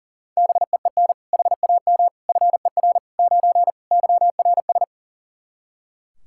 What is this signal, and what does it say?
Morse code, heard as a keyed tone, about 37 wpm, 700 Hz. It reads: BEEN HAM FER 9 YRS
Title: Practice QSO - Exchange 3 - More Info - 40wpm